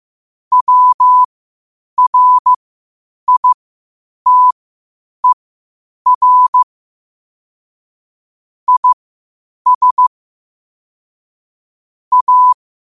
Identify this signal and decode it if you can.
Morse code, heard as a keyed tone, about 15 wpm, spaced as for 8 wpm, 1000 Hz. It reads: WRITER IS A
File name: qotdfiles / 8wpm